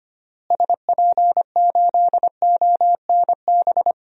Morse code, heard as a keyed tone, about 25 wpm, 700 Hz. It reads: SP8ONB
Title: DXCC-100 Call Signs - 6 Characters - Repeated 3x - 25wpm